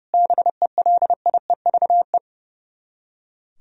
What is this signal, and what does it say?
Morse code, heard as a keyed tone, about 37 wpm, 700 Hz. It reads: BELIEVE
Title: Speed-Racing - Sets of 4 Words 30wpm